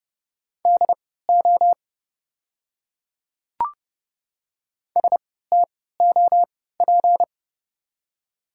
Morse code, heard as a keyed tone, about 30 wpm, spaced as for 17 wpm, 700 Hz. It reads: DO STOP
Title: 3x Character Spacing - Top 300 Words - 30wpm